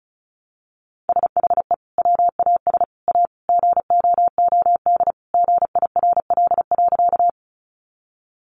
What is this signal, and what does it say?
Morse code, heard as a keyed tone, about 34 wpm, 700 Hz. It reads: SHE WAS A GOOD GIRL.
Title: Sentences from Top 400 Words 35wpm